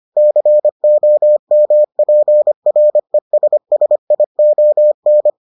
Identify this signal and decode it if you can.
Morse code, heard as a keyed tone, about 25 wpm, 600 Hz. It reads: COMPRESSION